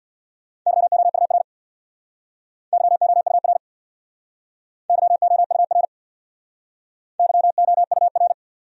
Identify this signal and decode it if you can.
Morse code, heard as a keyed tone, about 66 wpm, 700 Hz. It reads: YOUR YOUR YOUR YOUR